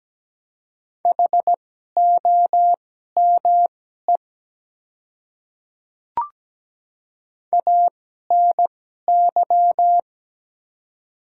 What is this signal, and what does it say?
Morse code, heard as a keyed tone, about 17 wpm, spaced as for 12 wpm, 700 Hz. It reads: HOME ANY
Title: 2x Character Spacing - Top 200 Words - 17wpm